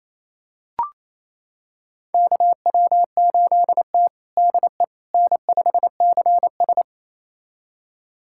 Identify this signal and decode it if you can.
Morse code, heard as a keyed tone, about 28 wpm, 700 Hz. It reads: KW8T DE N5CS
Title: USA Callsign DE Callsign - 4 Characters - 28wpm